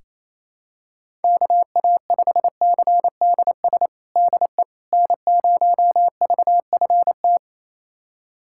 Morse code, heard as a keyed tone, about 28 wpm, 700 Hz. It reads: KA5CDS DE N0VFT
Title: USA Callsign DE Callsign 28wpm